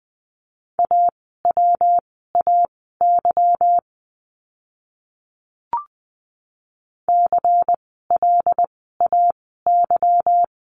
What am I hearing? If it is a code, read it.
Morse code, heard as a keyed tone, about 20 wpm, spaced as for 15 wpm, 700 Hz. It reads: AWAY CLAY
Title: 146 Lesson 24f - Letter y - Words 20wpm